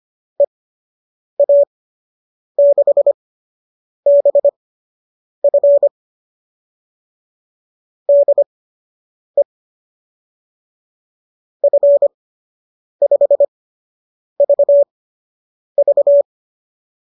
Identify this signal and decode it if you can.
Morse code, heard as a keyed tone, about 25 wpm, spaced as for 8 wpm, 570 Hz. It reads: EA6BF DE F5VV